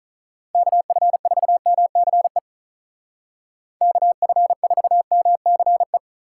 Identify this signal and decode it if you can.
Morse code, heard as a keyed tone, about 44 wpm, 700 Hz. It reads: KF4MCE KF4MCE